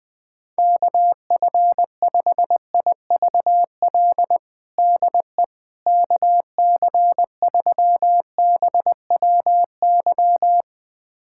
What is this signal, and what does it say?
Morse code, heard as a keyed tone, about 20 wpm, 700 Hz. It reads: KF5IVL DE KC3BWY
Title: USA Callsign DE Callsign - 6 Characters - 20wpm